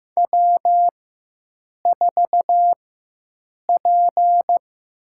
Morse code, heard as a keyed tone, about 15 wpm, spaced as for 7 wpm, 700 Hz. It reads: W4P
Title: USA Call Signs - 4 Characters - 4x Character Spacing - 15wpm